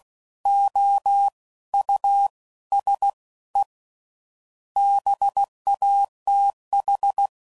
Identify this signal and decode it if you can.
Morse code, heard as a keyed tone, about 16 wpm, 800 Hz. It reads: O U S E BATH
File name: Suffixes / General / Houses